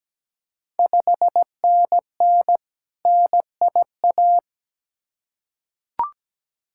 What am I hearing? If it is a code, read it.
Morse code, heard as a keyed tone, about 17 wpm, 690 Hz. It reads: 5NN NIA
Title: Ontario QSO Party Exchange 17wpm